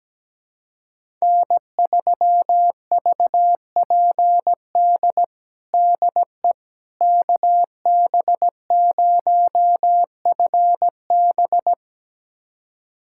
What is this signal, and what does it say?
Morse code, heard as a keyed tone, about 17 wpm, 700 Hz. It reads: N3VPD DE KB0FB
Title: USA Callsign DE Callsign - 5 Characters - 17wpm